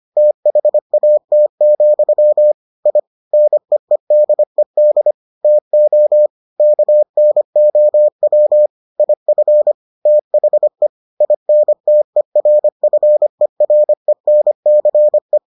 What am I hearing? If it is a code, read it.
Morse code, heard as a keyed tone, about 25 wpm, 600 Hz. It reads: THAT, I NEEDED TO KNOW IF THE INTERFERENCE